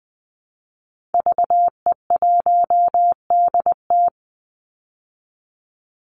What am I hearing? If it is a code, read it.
Morse code, heard as a keyed tone, about 20 wpm, 700 Hz. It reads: VE1DT
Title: Call Signs Repeated 3x - 20wpm